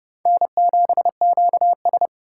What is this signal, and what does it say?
Morse code, heard as a keyed tone, about 36 wpm, 700 Hz. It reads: N7QS
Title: Speed Racing - USA 1x2 Call Signs 30wpm